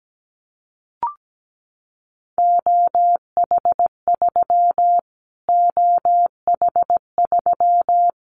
Morse code, heard as a keyed tone, about 17 wpm, 700 Hz. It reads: OH3 OH3
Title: Callsign Prefixes for Top 25 DXCC - Repeated 3x - 17wpm